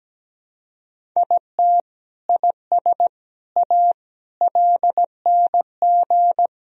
Morse code, heard as a keyed tone, about 17 wpm, 700 Hz. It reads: IT IS A LNG
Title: Sentences from Top 300 Words and Abbreviations 17wpm